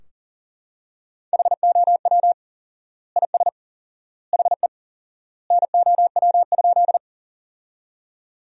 Morse code, heard as a keyed tone, about 40 wpm, 700 Hz. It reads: HOW IS HE NOW?
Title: 4x Word-Spacing - Sentences from Top 100 Words - 40wpm